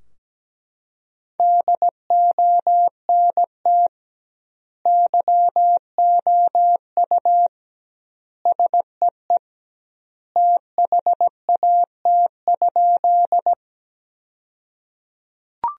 Morse code, heard as a keyed tone, about 17 wpm, 700 Hz. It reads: DONT YOU SEE THAT?
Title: 2x Word-Spacing - Sentences from Top 300 Words - 17wpm